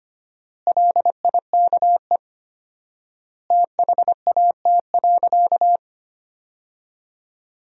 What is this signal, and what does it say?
Morse code, heard as a keyed tone, about 25 wpm, 700 Hz. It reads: LIKE THAT.
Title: 4x Word-Spacing - Sentences from Top 500 Words - 25wpm